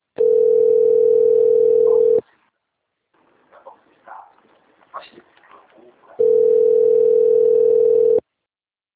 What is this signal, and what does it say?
Ringback tone (North America), 440 + 480 Hz, with speech.
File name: rg-1600-unknown- - - .28994